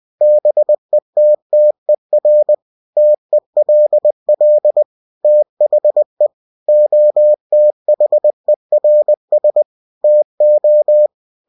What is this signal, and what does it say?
Morse code, heard as a keyed tone, about 20 wpm, 600 Hz. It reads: BETTER TELL THE OTHERS TO
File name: cw-books / winnie-the-pooh / winnie-20wpm-page-0077